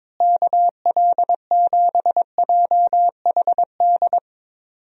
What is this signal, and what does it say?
Morse code, heard as a keyed tone, about 21 wpm, 700 Hz. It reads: KL7JHD
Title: USA Call Signs - 6 Characters - Repeated 3x - 22wpm